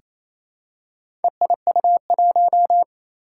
Morse code, heard as a keyed tone, about 27 wpm, 700 Hz. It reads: EIU1
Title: Four Challenging Letter-Number Combinations 28wpm